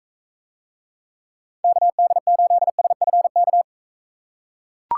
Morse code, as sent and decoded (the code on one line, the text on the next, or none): -.- -.. ---.. ... ..-. -.-
KD8SFK